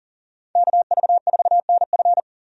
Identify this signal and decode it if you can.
Morse code, heard as a keyed tone, about 40 wpm, 700 Hz. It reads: KV4NF
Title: USA Call Signs - 5 Characters - 40wpm